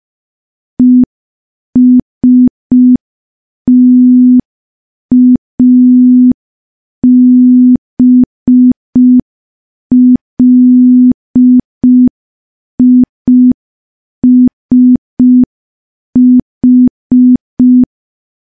Morse code, heard as a keyed tone, about 5 wpm, 250 Hz. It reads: ESTABLISH
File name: morse / morse-training-05WPM